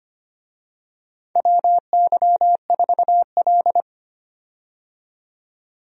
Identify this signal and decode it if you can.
Morse code, heard as a keyed tone, about 25 wpm, 700 Hz. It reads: WY4L
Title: USA Call Signs - 4 Characters - 25wpm